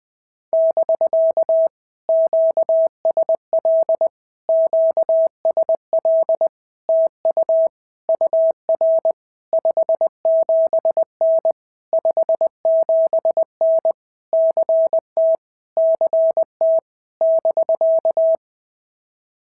Morse code, heard as a keyed tone, about 20 wpm, 650 Hz. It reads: <BK> QSL QSL TU UR 57N 57N CT CT <BK>
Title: POTA QSO Practice - 20wpm